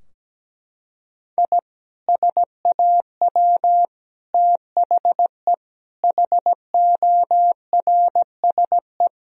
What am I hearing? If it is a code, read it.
Morse code, heard as a keyed tone, about 17 wpm, 700 Hz. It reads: I SAW THE HORSE
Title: Sentences from Top 1000 Words - Review Part 2 - 17wpm